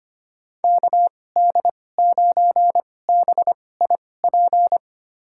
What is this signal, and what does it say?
Morse code, heard as a keyed tone, about 25 wpm, spaced as for 18 wpm, 700 Hz. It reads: KD9BIP
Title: USA Call Signs - 6 Characters - 2x Character Spacing - 25wpm